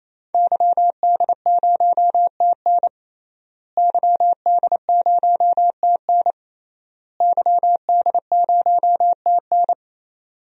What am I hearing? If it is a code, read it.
Morse code, heard as a keyed tone, about 27 wpm, 700 Hz. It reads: YD0TN YD0TN YD0TN
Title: DXCC-100 Call Signs - 5 Characters - Repeated 3x - 28wpm